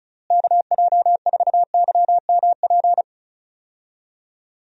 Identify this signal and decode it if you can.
Morse code, heard as a keyed tone, about 36 wpm, 700 Hz. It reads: KJ4YMP
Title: USA Call Signs - 6 Characters - Repeated 3x - 35wpm